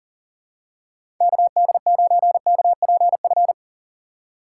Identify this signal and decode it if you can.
Morse code, heard as a keyed tone, about 39 wpm, 700 Hz. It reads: KD9KPF
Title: USA Call Signs - 6 Characters - 6x Character Spacing - 40wpm